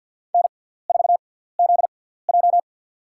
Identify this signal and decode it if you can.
Morse code, heard as a keyed tone, about 51 wpm, spaced as for 17 wpm, 700 Hz. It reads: N4ZJ
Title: USA Call Signs - 4 Characters - 6x Character Spacing - 50wpm